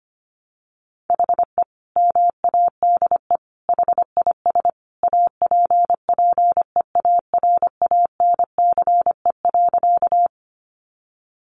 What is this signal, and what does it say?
Morse code, heard as a keyed tone, about 25 wpm, 700 Hz. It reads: HE MADE HIS APPEARANCE.
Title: Sentences from Top 1100 Words 25wpm